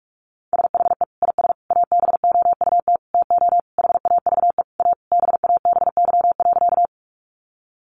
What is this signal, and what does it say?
Morse code, heard as a keyed tone, about 44 wpm, 700 Hz. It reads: SHE IS ABOUT TO HAVE A BABY.
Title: Sentences from Top 1300 Words 45wpm